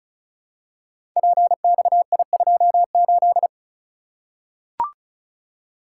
Morse code, heard as a keyed tone, about 36 wpm, 700 Hz. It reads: PXI28
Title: Code Groups - 5 Letter:Numbers - 35wpm